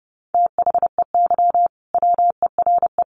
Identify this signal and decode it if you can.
Morse code, heard as a keyed tone, about 30 wpm, 700 Hz. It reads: THEY WERE
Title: Sentences from Top 1400 Words 30wpm